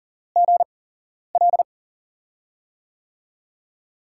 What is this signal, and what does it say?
Morse code, heard as a keyed tone, about 43 wpm, spaced as for 11 wpm, 690 Hz. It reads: GL
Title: USA Call Signs - 8x Character Spacing - 40wpm